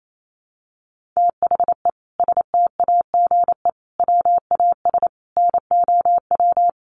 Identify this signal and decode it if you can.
Morse code, heard as a keyed tone, about 28 wpm, 700 Hz. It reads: THE STAGE WAS NOW
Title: Sentences from Top 1500 Words 28wpm